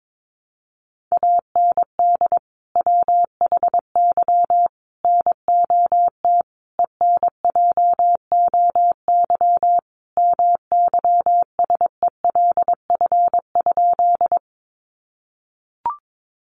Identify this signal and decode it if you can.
Morse code, heard as a keyed tone, about 22 wpm, 700 Hz. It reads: AND WHY NOT ENJOY MYSELF?
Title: Sentences from Top 1300 Words 22wpm